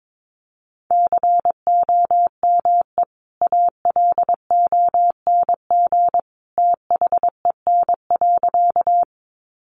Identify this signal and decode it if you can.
Morse code, heard as a keyed tone, about 22 wpm, 700 Hz. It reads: COME ALONG THEN.
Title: Sentences from Top 700 Words 22wpm